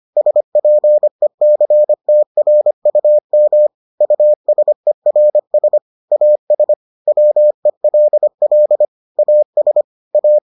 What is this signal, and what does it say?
Morse code, heard as a keyed tone, about 25 wpm, 600 Hz. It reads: SPECTRUM USERS AS WELL AS A